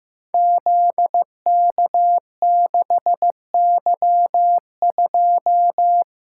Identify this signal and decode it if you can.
Morse code, heard as a keyed tone, about 15 wpm, 700 Hz. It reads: ZK6Y2
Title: Code Groups - 5 Letter:Numbers - 15wpm